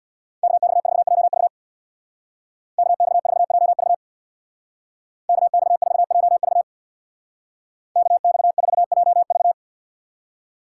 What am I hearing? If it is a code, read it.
Morse code, heard as a keyed tone, about 61 wpm, 700 Hz. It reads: KX4JV KX4JV KX4JV KX4JV
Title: Speed Racing - USA 2x2 Call Signs 50wpm